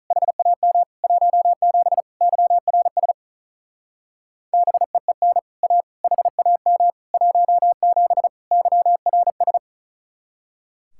Morse code, heard as a keyed tone, about 44 wpm, 700 Hz. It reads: HAM 17 YRS BEEN A HAM 17 YRS
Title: Speed-Racing Common QSO Phrases - 35wpm